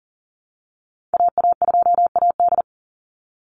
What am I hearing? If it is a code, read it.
Morse code, heard as a keyed tone, about 40 wpm, 700 Hz. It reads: AA2AD
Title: Call Signs Repeated 2x - 40wpm